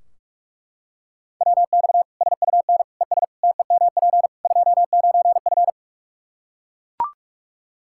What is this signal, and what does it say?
Morse code, heard as a keyed tone, about 43 wpm, 700 Hz. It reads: WX SUN ES TEMP 29F
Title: K1USN SST QSO Practice 45wpm